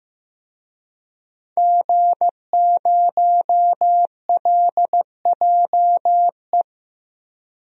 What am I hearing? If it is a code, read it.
Morse code, heard as a keyed tone, about 15 wpm, 700 Hz. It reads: G0LJE
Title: DXCC-100 Call Signs - 5 Characters - Repeated 2x - 15wpm